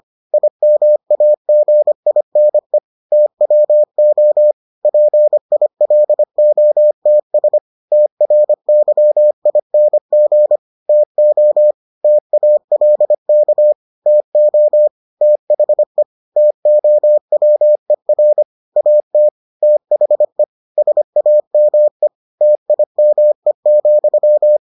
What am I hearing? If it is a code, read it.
Morse code, heard as a keyed tone, about 25 wpm, 600 Hz. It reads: IMAGINE TWO PILOTS TRYING TO TALK TO THE TOWER AT THE SAME TIME,